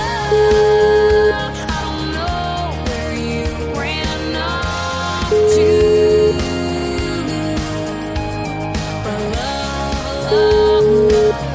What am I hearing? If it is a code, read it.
Ringback tone (Europe, 425 Hz), with noise.